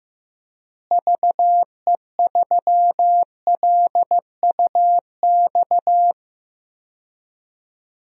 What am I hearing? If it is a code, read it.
Morse code, heard as a keyed tone, about 15 wpm, 700 Hz. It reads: VE3LUX
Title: DXCC-100 Call Signs - 6 Characters - Repeated 3x - 15wpm